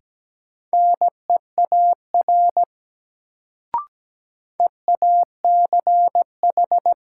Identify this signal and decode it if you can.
Morse code, heard as a keyed tone, about 17 wpm, 700 Hz. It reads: NEAR EACH
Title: Sentences from Top 200 Words - New Word Drill - 17wpm